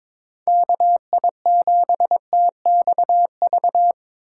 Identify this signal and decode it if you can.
Morse code, heard as a keyed tone, about 21 wpm, 700 Hz. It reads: KI7TXV
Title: USA Call Signs - 6 Characters - 3x Character Spacing - 22wpm